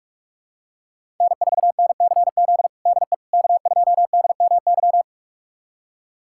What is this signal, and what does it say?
Morse code, heard as a keyed tone, about 46 wpm, 700 Hz. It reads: N4NCZ DE K2DMY
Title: USA Callsign DE Callsign - 5 Characters - 45wpm